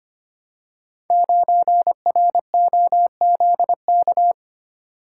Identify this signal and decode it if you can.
Morse code, heard as a keyed tone, about 25 wpm, 700 Hz. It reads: 9ROZK
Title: Code Groups - 5 Letter:Numbers - 25wpm